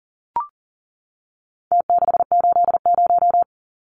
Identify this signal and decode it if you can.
Morse code, heard as a keyed tone, about 40 wpm, 700 Hz. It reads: T680
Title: Numbers - Contest Style - 4-Digit 40wpm